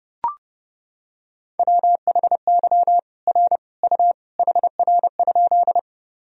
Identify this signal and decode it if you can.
Morse code, heard as a keyed tone, about 32 wpm, 700 Hz. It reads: WHY R U HR?
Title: Sentences from Top 200 Words and Abbreviations 30wpm